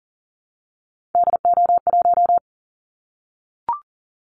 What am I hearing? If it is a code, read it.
Morse code, heard as a keyed tone, about 40 wpm, 700 Hz. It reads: DO1
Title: Callsign Prefixes for Top 25 DXCC - Repeated 3x - 40wpm